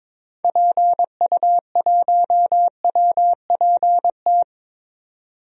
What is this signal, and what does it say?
Morse code, heard as a keyed tone, about 21 wpm, 700 Hz. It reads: PU1WPT
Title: DXCC-100 Call Signs - 6 Characters - 22wpm